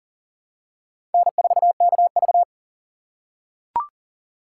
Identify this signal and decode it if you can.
Morse code, heard as a keyed tone, about 40 wpm, 700 Hz. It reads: N4KV
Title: USA Call Signs - 4 Characters - 2x Character Spacing - 40wpm